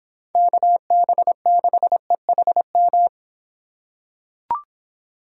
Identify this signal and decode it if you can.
Morse code, heard as a keyed tone, about 26 wpm, 700 Hz. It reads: KB6EHM